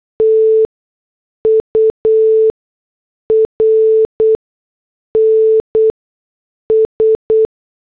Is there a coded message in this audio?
Yes: Morse code from 0.2 to 7.5 s, heard as a keyed tone, about 8 wpm, spaced as for 6 wpm, 440 Hz. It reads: TURNS